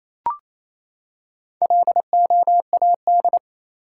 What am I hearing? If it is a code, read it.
Morse code, heard as a keyed tone, about 28 wpm, 700 Hz. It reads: LOAD
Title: Four Letter Words - 28wpm